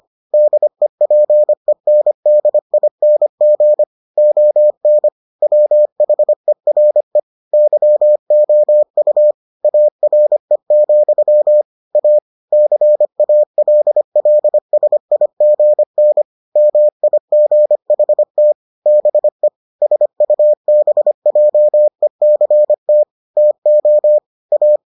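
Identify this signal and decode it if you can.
Morse code, heard as a keyed tone, about 25 wpm, 600 Hz. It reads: DEPENDING ON WHERE YOU ARE, A CALLSIGN MIGHT BE SUBJECT TO A
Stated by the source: Morse edition of the Foundations of Amateur Radio episodes.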